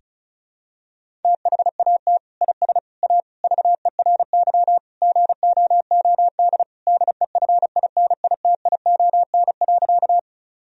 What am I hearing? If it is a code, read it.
Morse code, heard as a keyed tone, about 36 wpm, 700 Hz. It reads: THAT IS A VERY GOOD DEFINITION.
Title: Sentences from Top 1000 Words - Review Part 4 - 35wpm